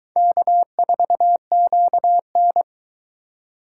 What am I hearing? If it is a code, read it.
Morse code, heard as a keyed tone, about 23 wpm, 700 Hz. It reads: K4QN